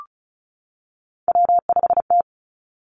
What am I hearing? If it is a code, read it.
Morse code, heard as a keyed tone, about 36 wpm, 700 Hz. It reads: W5T
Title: Three Challenging Letter-Number Combinations 35wpm